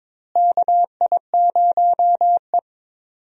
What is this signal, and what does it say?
Morse code, heard as a keyed tone, about 21 wpm, 700 Hz. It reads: KI0E
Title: USA Call Signs - 4 Characters - Repeated 3x - 22wpm